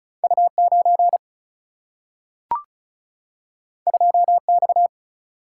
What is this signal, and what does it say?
Morse code, heard as a keyed tone, about 34 wpm, 700 Hz. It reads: U9 2X
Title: Single-Letter Single-Number 35wpm